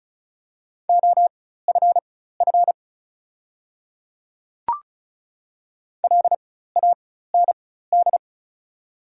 Morse code, heard as a keyed tone, about 34 wpm, spaced as for 16 wpm, 700 Hz. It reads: OFF LAND
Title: 4x Character Spacing - Top 300 Words - 35wpm